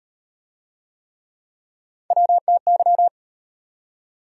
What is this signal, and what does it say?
Morse code, heard as a keyed tone, about 38 wpm, 700 Hz. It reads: WTY